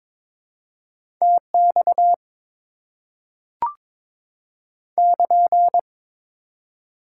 Morse code, heard as a keyed tone, about 21 wpm, 690 Hz. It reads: TX <KN>
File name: Top 90 QSO Elements - 4x Character Spacing - 22wpm